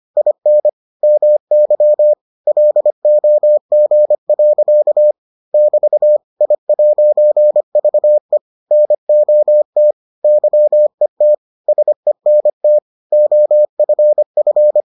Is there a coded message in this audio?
Yes: Morse code from 0.2 to 14.8 s, heard as a keyed tone, about 25 wpm, 600 Hz. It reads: IN MY LOG. <BT> I'VE NOT YET SENT OFF